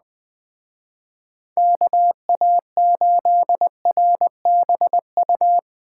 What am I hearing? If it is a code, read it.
Morse code, heard as a keyed tone, about 20 wpm, 700 Hz. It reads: KA8RBU
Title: Speed Racing - USA 2x3 Call Signs 17wpm